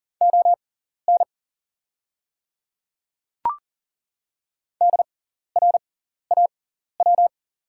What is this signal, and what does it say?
Morse code, heard as a keyed tone, about 40 wpm, spaced as for 14 wpm, 690 Hz. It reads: ON DRAW
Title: 6x Character Spacing - Top 500 Words - 40wpm